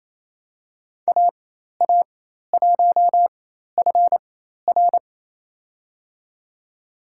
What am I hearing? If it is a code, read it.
Morse code, heard as a keyed tone, about 28 wpm, spaced as for 13 wpm, 700 Hz. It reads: AA1FR
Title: USA Call Signs - 4x Character Spacing - 28wpm